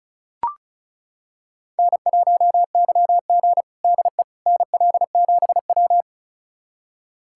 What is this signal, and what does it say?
Morse code, heard as a keyed tone, about 36 wpm, 700 Hz. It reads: N1YG DE NL7W
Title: USA Callsign DE Callsign - 4 Characters - 35wpm